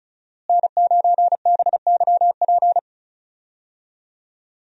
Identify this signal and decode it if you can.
Morse code, heard as a keyed tone, about 34 wpm, 700 Hz. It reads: N9BYP
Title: USA Call Signs - 5 Characters - Repeated 2x - 35wpm